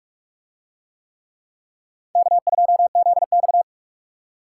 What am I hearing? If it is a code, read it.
Morse code, heard as a keyed tone, about 44 wpm, 700 Hz. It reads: K2ZX